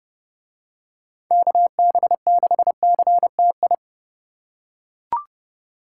Morse code, heard as a keyed tone, about 30 wpm, 700 Hz. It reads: KB6CTI